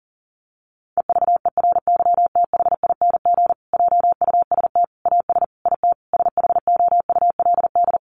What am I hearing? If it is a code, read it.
Morse code, heard as a keyed tone, about 40 wpm, 700 Hz. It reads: EVERYTHING JUST AS IT SHOULD
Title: Sentences from Top 700 Words 40wpm